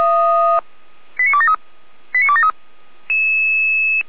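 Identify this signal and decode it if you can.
Five-tone selective call (ZVEI), heard as quick runs of five tones, tones of about 70 ms. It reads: 89273 89273
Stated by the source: ZVEI five-tone call 89150 received on SDR.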